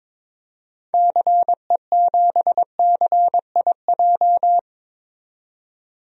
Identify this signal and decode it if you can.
Morse code, heard as a keyed tone, about 22 wpm, 700 Hz. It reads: CE7CIJ
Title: DXCC-100 Call Signs - 6 Characters - 22wpm